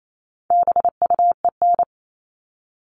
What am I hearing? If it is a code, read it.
Morse code, heard as a keyed tone, about 28 wpm, 700 Hz. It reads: BUEN
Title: All Cut Numbers - 4-Digit 28wpm